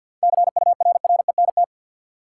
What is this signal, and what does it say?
Morse code, heard as a keyed tone, about 50 wpm, 700 Hz. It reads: CURRENT
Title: Sentences from Top 1000 Words - Review Part 4 - New Word Drill - 50wpm